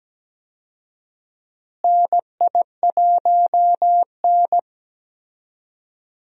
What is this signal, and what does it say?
Morse code, heard as a keyed tone, about 17 wpm, 700 Hz. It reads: NI1N
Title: USA Call Signs - 4 Characters - Repeated 3x - 17wpm